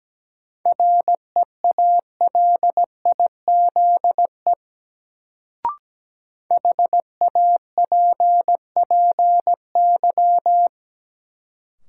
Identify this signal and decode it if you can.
Morse code, heard as a keyed tone, about 17 wpm, 700 Hz. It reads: REALIZE HAPPY
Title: Sentences from Top 1000 Words - New Word Drill - 17wpm